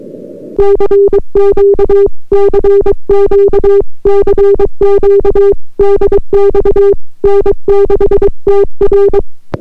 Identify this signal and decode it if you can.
Morse code, heard as a keyed tone, about 22 wpm, 400 Hz. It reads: CQ CQ CQ DX N6TR